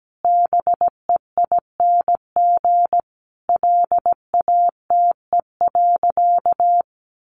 Morse code, heard as a keyed tone, about 17 wpm, 700 Hz. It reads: BEING LATE.
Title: Sentences from Top 1500 Words 17wpm